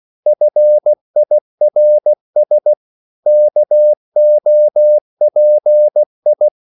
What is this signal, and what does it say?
Morse code, heard as a keyed tone, about 16 wpm, 600 Hz. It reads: FIRS KOPI
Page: Korte danske ord 16wpm